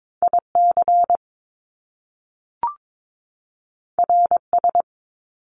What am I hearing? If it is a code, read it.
Morse code, heard as a keyed tone, about 22 wpm, 700 Hz. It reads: IC RS
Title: Top Two Letter Combinations in Words 22wpm